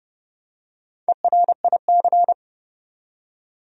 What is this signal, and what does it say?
Morse code, heard as a keyed tone, about 30 wpm, 700 Hz. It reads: ERIC
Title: USA Common Names - Max 4 Letters - 30wpm